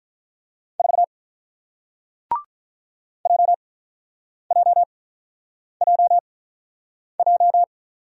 Morse code, heard as a keyed tone, about 46 wpm, spaced as for 9 wpm, 700 Hz. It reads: 4 JJJJ